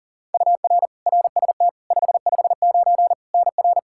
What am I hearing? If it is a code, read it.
Morse code, heard as a keyed tone, about 40 wpm, 700 Hz. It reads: UR RST 559 NR
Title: Practice QSO - Exchange 1 - Beginning - 40wpm